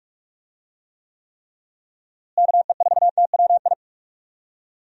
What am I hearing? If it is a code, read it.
Morse code, heard as a keyed tone, about 45 wpm, 700 Hz. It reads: KE4TWI
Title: USA Call Signs - 6 Characters - Repeated 3x - 45wpm